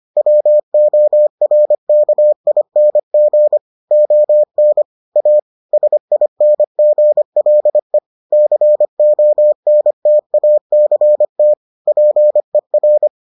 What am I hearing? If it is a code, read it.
Morse code, heard as a keyed tone, about 25 wpm, 600 Hz. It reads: WORKING ON A SINGLE CONTACT PER